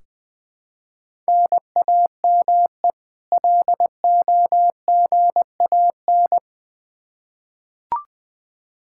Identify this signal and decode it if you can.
Morse code, heard as a keyed tone, about 20 wpm, 700 Hz. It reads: NAME LOGAN
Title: Practice QSO - Exchange 1 - Beginning - 20wpm